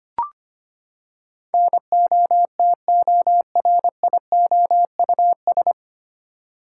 Morse code, heard as a keyed tone, about 25 wpm, 700 Hz. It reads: NOTORIOUS
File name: Nine Letter Words - 25wpm